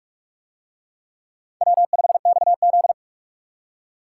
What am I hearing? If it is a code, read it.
Morse code, heard as a keyed tone, about 40 wpm, 700 Hz. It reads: W5XZ